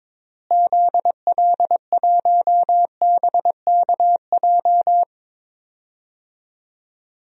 Morse code, heard as a keyed tone, about 22 wpm, 700 Hz. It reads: ZL1BKJ
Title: DXCC-100 Call Signs - 6 Characters - Repeated 2x - 22wpm